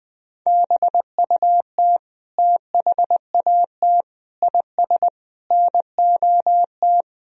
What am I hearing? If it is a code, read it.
Morse code, heard as a keyed tone, about 21 wpm, 700 Hz. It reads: BUT THAT IS NOT
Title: Sentences from Top 1000 Words - Review Part 4 - 20wpm